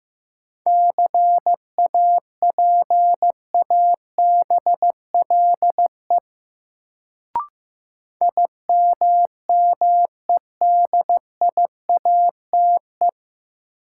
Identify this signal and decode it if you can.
Morse code, heard as a keyed tone, about 15 wpm, 700 Hz. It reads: CAPABLE IMMEDIATE
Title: Sentences from Top 1200 Words - New Word Drill Alt - 15wpm